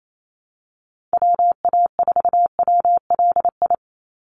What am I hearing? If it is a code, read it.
Morse code, heard as a keyed tone, about 28 wpm, 700 Hz. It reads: WA4WLI
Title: Call Signs - 28wpm